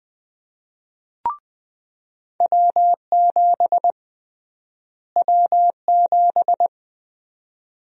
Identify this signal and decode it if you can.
Morse code, heard as a keyed tone, about 20 wpm, 700 Hz. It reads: W7 W7
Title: USA Call Sign Prefixes - Repeated 2x - 20wpm